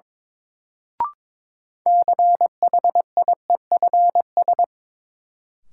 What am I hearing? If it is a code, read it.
Morse code, heard as a keyed tone, about 22 wpm, 700 Hz. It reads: CHIEFS
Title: Sets of 5 Words Encore - New Word Drill - 22wpm